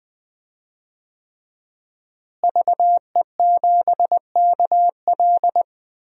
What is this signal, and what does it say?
Morse code, heard as a keyed tone, about 20 wpm, 700 Hz. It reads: VE7KL
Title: DXCC-100 Call Signs - 5 Characters - Repeated 3x - 20wpm